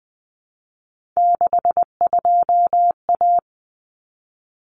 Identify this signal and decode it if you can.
Morse code, heard as a keyed tone, about 20 wpm, 700 Hz. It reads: 62A
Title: Common Cut Numbers - 3-Digit 20wpm